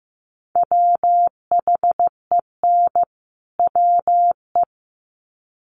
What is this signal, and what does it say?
Morse code, heard as a keyed tone, about 15 wpm, 700 Hz. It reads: WHEN WE
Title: Sets of 2 Words 15wpm